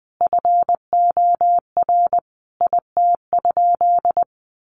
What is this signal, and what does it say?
Morse code, heard as a keyed tone, about 20 wpm, 700 Hz. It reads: FOR IT?
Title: Sentences from Top 900 Words 20wpm